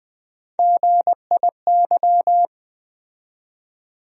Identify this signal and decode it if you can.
Morse code, heard as a keyed tone, about 20 wpm, 700 Hz. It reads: GIY
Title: USA Call Sign Postfixes - 3 Letters 20wpm